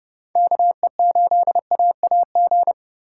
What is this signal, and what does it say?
Morse code, heard as a keyed tone, about 36 wpm, 700 Hz. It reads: KE8AAG
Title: Speed Racing - USA Call Signs 30wpm